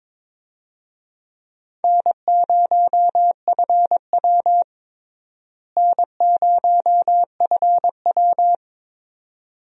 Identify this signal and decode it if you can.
Morse code, heard as a keyed tone, about 21 wpm, 700 Hz. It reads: N0FW N0FW
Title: USA Call Signs - 4 Characters - Repeated 2x - 22wpm